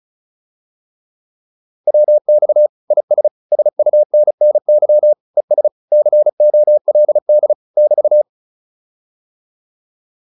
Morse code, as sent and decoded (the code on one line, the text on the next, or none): .-- -..- / .. ... / ... ..- -. -. -.-- / . ... / -.-. --- .-.. -.. / -...-
WX IS SUNNY ES COLD <BT>